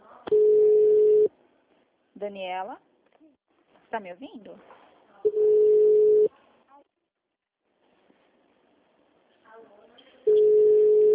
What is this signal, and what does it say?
Ringback tone (Europe, 425 Hz), with speech.
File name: out- -2011- - - .16